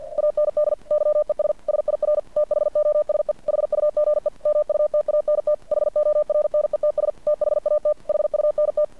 Morse code, heard as a keyed tone, about 50 wpm, 600 Hz. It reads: AND YES SIR THOSE HUGE MUTANT HORNETS THAT HUNT
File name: archived / CRSnet